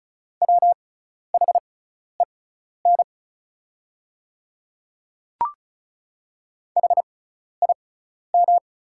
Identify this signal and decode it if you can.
Morse code, heard as a keyed tone, about 36 wpm, spaced as for 12 wpm, 700 Hz. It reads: WHEN HIM
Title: 6x Character Spacing - Top 100 Words - 35wpm